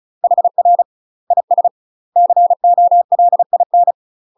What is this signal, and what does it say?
Morse code, heard as a keyed tone, about 35 wpm, 700 Hz. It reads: HR IS COLIN
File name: UK MASTER QSOs TRAINING / 35WPM / UK MASTER QSOs 0029